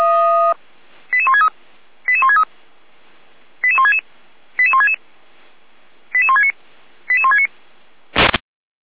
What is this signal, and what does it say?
Five-tone selective call (ZVEI), heard as quick runs of five tones, tones of about 70 ms. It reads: 88163 88163 88166 88166 88169 88169
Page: SDR/ZVEI